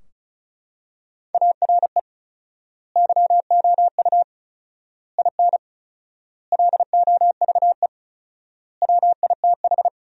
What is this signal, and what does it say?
Morse code, heard as a keyed tone, about 36 wpm, 700 Hz. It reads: ARE YOU IN LOVE WITH
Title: 4x Word-Spacing - Sentences from Top 400 Words - 35wpm